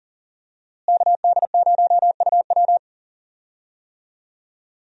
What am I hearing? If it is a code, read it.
Morse code, heard as a keyed tone, about 38 wpm, 700 Hz. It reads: KD0UW
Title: USA Call Signs - 5 Characters - Repeated 2x - 40wpm